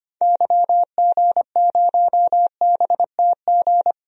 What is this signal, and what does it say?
Morse code, heard as a keyed tone, about 25 wpm, 700 Hz. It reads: YG0BTG
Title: DXCC-100 Call Signs - 6 Characters - 25wpm